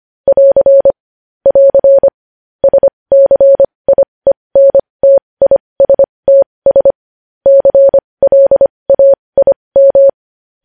Morse code, heard as a keyed tone, about 25 wpm, spaced as for 20 wpm, 550 Hz. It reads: <AR> <AR> SCIENTISTS CLAIM